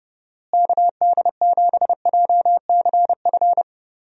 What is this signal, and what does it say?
Morse code, heard as a keyed tone, about 32 wpm, 700 Hz. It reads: KD7JCF
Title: USA Call Signs - 6 Characters - Repeated 2x - 30wpm